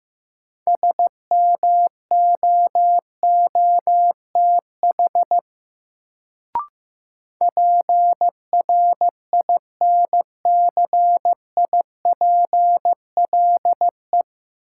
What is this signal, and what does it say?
Morse code, heard as a keyed tone, about 15 wpm, 700 Hz. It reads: SMOOTH PRINCIPLE
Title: Sentences from Top 1400 Words - New Word Drill Alt - 15wpm